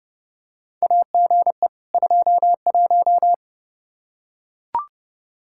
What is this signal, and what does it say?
Morse code, heard as a keyed tone, about 30 wpm, 700 Hz. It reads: AGE 21
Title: Practice QSO - Exchange 3 - More Info - 30wpm